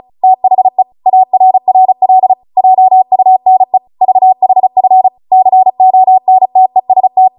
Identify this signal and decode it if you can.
Morse code, heard as a keyed tone, about 36 wpm, 750 Hz. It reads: THE ARRL JUNE VHF CONTEST